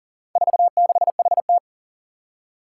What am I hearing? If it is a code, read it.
Morse code, heard as a keyed tone, about 40 wpm, 700 Hz. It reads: 46HT
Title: Four Challenging Letter-Number Combinations 40wpm